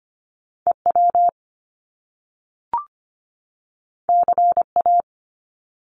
Morse code, heard as a keyed tone, about 25 wpm, 700 Hz. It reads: EW CA
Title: Top Two Letter Combinations in Words 25wpm